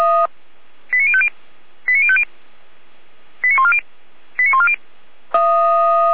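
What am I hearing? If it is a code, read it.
Five-tone selective call (ZVEI), heard as quick runs of five tones, tones of about 70 ms. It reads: 89050 89050 89140 89140
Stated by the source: ZVEI five-tone call received on SDR.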